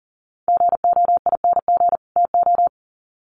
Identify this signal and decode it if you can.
Morse code, heard as a keyed tone, about 40 wpm, 700 Hz. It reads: GOING TO
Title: Sets of 2 Words 40wpm